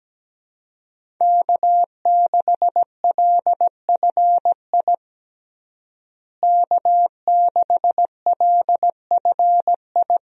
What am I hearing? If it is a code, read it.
Morse code, heard as a keyed tone, about 17 wpm, 700 Hz. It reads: K6LFI K6LFI
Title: USA Call Signs - 5 Characters - Repeated 3x - 17wpm